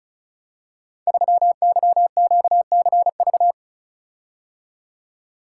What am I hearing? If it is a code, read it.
Morse code, heard as a keyed tone, about 36 wpm, 700 Hz. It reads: 3YQCV
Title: Code Groups - 5 Letter:Numbers - 35wpm